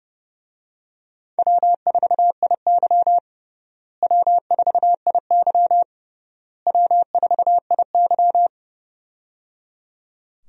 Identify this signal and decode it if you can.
Morse code, heard as a keyed tone, about 30 wpm, 700 Hz. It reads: W4IY W4IY W4IY
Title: USA Call Signs - 4 Characters - Repeated 3x - 30wpm